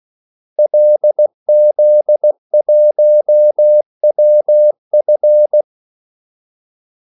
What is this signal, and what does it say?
Morse code, heard as a keyed tone, about 16 wpm, 600 Hz. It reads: LZ1WF
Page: Kaldesignaler 16 wpm